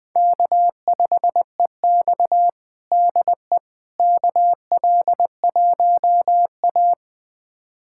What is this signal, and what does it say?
Morse code, heard as a keyed tone, about 21 wpm, 700 Hz. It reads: K5EX DE KL1A
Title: USA Callsign DE Callsign - 4 Characters - 20wpm